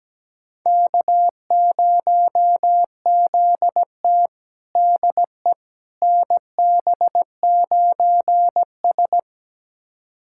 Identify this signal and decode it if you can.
Morse code, heard as a keyed tone, about 17 wpm, 700 Hz. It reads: K0ZT DE NB9S
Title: USA Callsign DE Callsign - 4 Characters - 17wpm